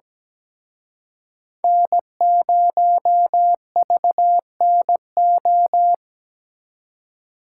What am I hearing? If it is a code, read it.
Morse code, heard as a keyed tone, about 20 wpm, 700 Hz. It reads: N0VNO
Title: Speed Racing - USA 1x3 Call Signs 17wpm